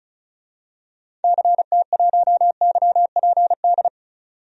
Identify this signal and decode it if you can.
Morse code, heard as a keyed tone, about 34 wpm, 700 Hz. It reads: CT1YPD
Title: DXCC-100 Call Signs - 6 Characters - 35wpm